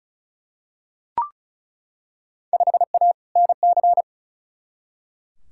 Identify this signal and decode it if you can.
Morse code, heard as a keyed tone, about 36 wpm, 700 Hz. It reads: 5A NC
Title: Field Day Exchanges - 35wpm